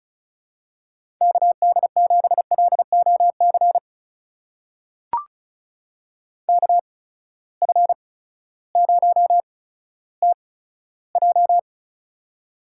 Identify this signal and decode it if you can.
Morse code, heard as a keyed tone, about 36 wpm, spaced as for 16 wpm, 700 Hz. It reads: KD7LOC KF0TJ
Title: USA Call Signs - 8x Character Spacing - 35wpm